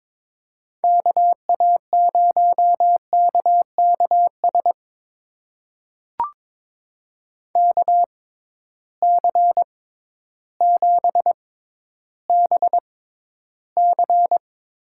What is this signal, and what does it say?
Morse code, heard as a keyed tone, about 21 wpm, spaced as for 12 wpm, 700 Hz. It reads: KA0KKS KC7BC
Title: USA Call Signs - 6x Character Spacing - 22wpm